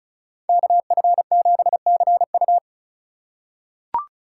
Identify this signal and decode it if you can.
Morse code, heard as a keyed tone, about 34 wpm, 700 Hz. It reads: KF7CU
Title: USA Call Signs - 2x Character Spacing - 35wpm